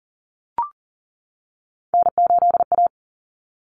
Morse code, heard as a keyed tone, about 38 wpm, 700 Hz. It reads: N8A
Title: Common Cut Numbers - 3-Digit 40wpm